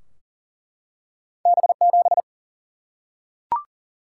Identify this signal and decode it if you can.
Morse code, heard as a keyed tone, about 38 wpm, 700 Hz. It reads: B7
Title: Single-Letter Single-Number 40wpm